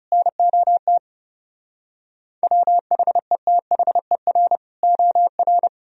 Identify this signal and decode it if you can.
Morse code, heard as a keyed tone, about 40 wpm, 700 Hz. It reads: NOT WHETHER OR